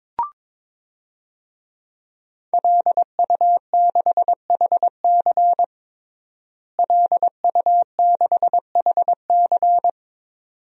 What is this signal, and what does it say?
Morse code, heard as a keyed tone, about 22 wpm, 700 Hz. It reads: LU6HC LU6HC
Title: DXCC-100 Call Signs - 5 Characters - Repeated 2x - 22wpm